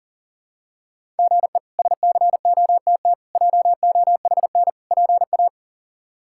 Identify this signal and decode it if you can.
Morse code, heard as a keyed tone, about 37 wpm, 700 Hz. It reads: GE SCOTT JOHN PA
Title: K1USN SST QSO Practice - 40wpm